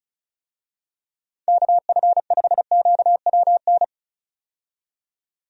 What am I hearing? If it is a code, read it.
Morse code, heard as a keyed tone, about 36 wpm, 700 Hz. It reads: KF5QWN
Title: USA Call Signs - 6 Characters - 35wpm